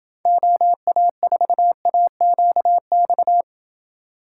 Morse code, heard as a keyed tone, about 29 wpm, 700 Hz. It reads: OA4AQX